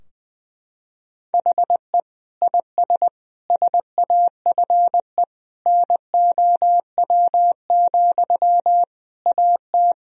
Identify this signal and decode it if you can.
Morse code, heard as a keyed tone, about 19 wpm, 700 Hz. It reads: HE IS SAFE NOW, AT
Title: Sentences from Top 1000 Words - Review Part 4 - 20wpm